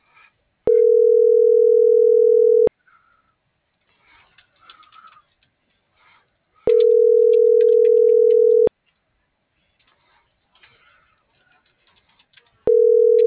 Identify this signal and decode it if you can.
Ringback tone (North America), 440 + 480 Hz.
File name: CENTRAL MAYAB / out raulcristinomayab - .24999